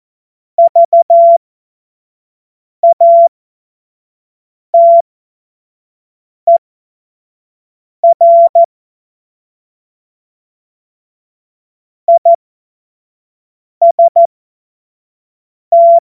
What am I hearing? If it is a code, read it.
Morse code, heard as a keyed tone, about 14 wpm, spaced as for 5 wpm, 680 Hz. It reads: VATER IST
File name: Morsekurs / Witze / Witz 15. August 2023 25 F70